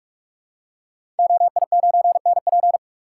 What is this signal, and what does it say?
Morse code, heard as a keyed tone, about 44 wpm, 700 Hz. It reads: OI9NP
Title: Code Groups - 5 Letter:Numbers - 45wpm